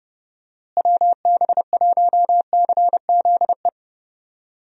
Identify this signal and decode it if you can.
Morse code, heard as a keyed tone, about 30 wpm, 700 Hz. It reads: WB1CZE
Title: USA Call Signs - 6 Characters - Repeated 3x - 30wpm